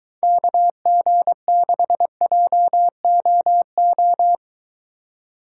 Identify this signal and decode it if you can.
Morse code, heard as a keyed tone, about 24 wpm, 700 Hz. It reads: KG6JOO